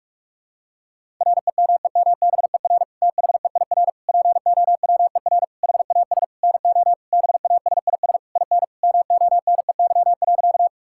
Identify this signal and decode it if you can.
Morse code, heard as a keyed tone, about 46 wpm, 690 Hz. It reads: REMEMBER THEIR POWER HAS NO BASIS IN MONEY.
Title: Sentences from Top 1000 Words - Review Part 5 - 45wpm